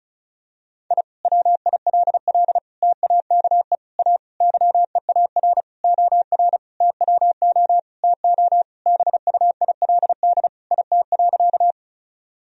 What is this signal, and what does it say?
Morse code, heard as a keyed tone, about 34 wpm, 700 Hz. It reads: I WILL TAKE A YEAR OR TWO TO BUILD IT.
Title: Sentences from Top 1000 Words - Review Part 1 - 35wpm